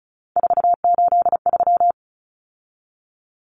Morse code, heard as a keyed tone, about 34 wpm, 700 Hz. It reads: 483
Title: Numbers - 3-Digit 35wpm